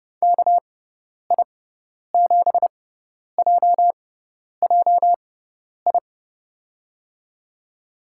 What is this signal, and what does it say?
Morse code, heard as a keyed tone, about 30 wpm, spaced as for 10 wpm, 700 Hz. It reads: KI7JJI
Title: USA Call Signs - 6x Character Spacing - 30wpm